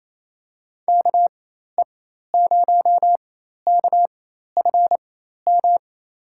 Morse code, heard as a keyed tone, about 27 wpm, spaced as for 13 wpm, 700 Hz. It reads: KE0KFM
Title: USA Call Signs - 4x Character Spacing - 28wpm